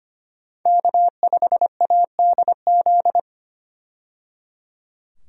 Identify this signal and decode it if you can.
Morse code, heard as a keyed tone, about 25 wpm, 700 Hz. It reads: K5ADZ
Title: USA Call Signs - 5 Characters - 25wpm